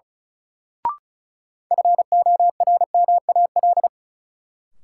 Morse code, heard as a keyed tone, about 36 wpm, 700 Hz. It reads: FORMAL
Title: Sentences from Top 1300 Words - New Word Drill - 35wpm